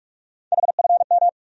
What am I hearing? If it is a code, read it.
Morse code, heard as a keyed tone, about 46 wpm, 700 Hz. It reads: HFM
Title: USA Call Sign Postfixes - 3 Letters 45wpm